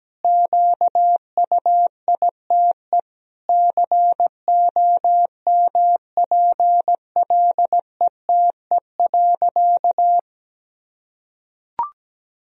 Morse code, heard as a keyed tone, about 17 wpm, 700 Hz. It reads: QUITE COMPLETE.
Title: Sentences from Top 1000 Words - Review Part 2 - 17wpm